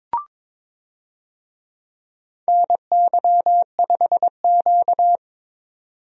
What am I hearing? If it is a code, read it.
Morse code, heard as a keyed tone, about 22 wpm, 700 Hz. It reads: NY5Q
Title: USA Call Signs - 4 Characters - 22wpm